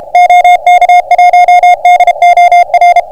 Morse code, heard as a keyed tone, about 28 wpm, 660 Hz, with noise. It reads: OK1DOR